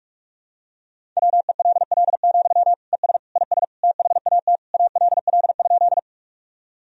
Morse code, heard as a keyed tone, about 42 wpm, 700 Hz. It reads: WELL, ES IS THAT ALL?
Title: Sentences from Top 200 Words and Abbreviations 45wpm